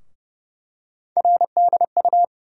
Morse code, heard as a keyed tone, about 29 wpm, 700 Hz. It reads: RDU
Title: USA Call Sign Postfixes - 3 Letters 30wpm